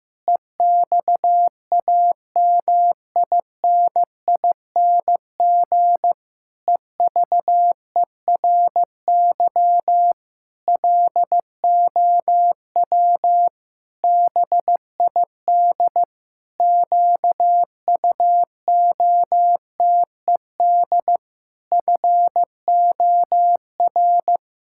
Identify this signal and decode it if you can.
Morse code, heard as a keyed tone, about 15 wpm, 700 Hz. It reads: EXAMINING EVERY LOW BID QUOTED FOR
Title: Pangram Challenge v2 - 15wpm